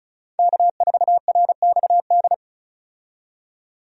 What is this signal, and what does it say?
Morse code, heard as a keyed tone, about 36 wpm, 700 Hz. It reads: K4RXD
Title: USA Call Signs - 5 Characters - 35wpm